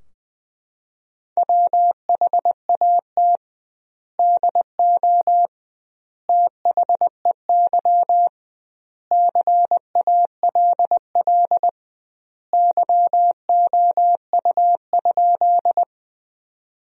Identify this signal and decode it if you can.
Morse code, heard as a keyed tone, about 21 wpm, 700 Hz. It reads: WHAT DO THEY CALL YOU?
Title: 2x Word-Spacing - Sentences from Top 100 Words - 20wpm